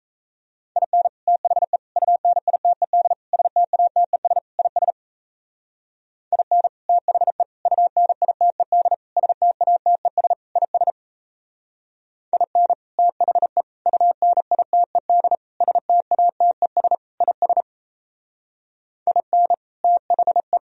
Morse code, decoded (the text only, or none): IN THE UNITED STATES IS IN THE UNITED STATES IS IN THE UNITED STATES IS IN THE